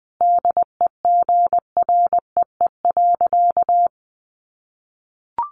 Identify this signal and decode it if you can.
Morse code, heard as a keyed tone, about 20 wpm, 700 Hz. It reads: DEGREE.
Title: Sentences from Top 1500 Words 20wpm